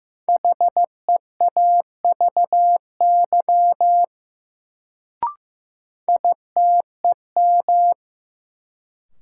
Morse code, heard as a keyed tone, about 15 wpm, 700 Hz. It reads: HEAVY ITEM
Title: Sentences from Top 1000 Words - Review Part 6 - New Word Drill - 15wpm